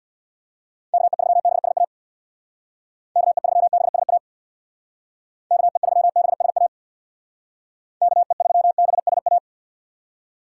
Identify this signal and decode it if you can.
Morse code, heard as a keyed tone, about 66 wpm, 700 Hz. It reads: KE3BSA KE3BSA KE3BSA KE3BSA